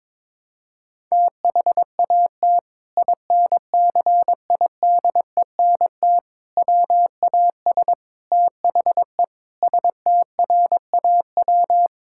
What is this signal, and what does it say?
Morse code, heard as a keyed tone, about 22 wpm, 700 Hz. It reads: THAT INCIDENT WAS THE STRAW
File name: practice / 22wpm / Sentences with 8-Word Phrases - 22wpm